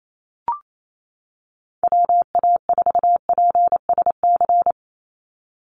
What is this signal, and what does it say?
Morse code, heard as a keyed tone, about 28 wpm, 700 Hz. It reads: WA4PSC
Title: Call Signs - 28wpm